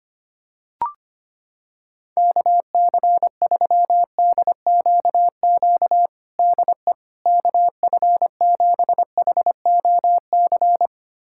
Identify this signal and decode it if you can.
Morse code, heard as a keyed tone, about 25 wpm, 700 Hz. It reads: KC3DQQ DE KF7HOC
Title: USA Callsign DE Callsign 25wpm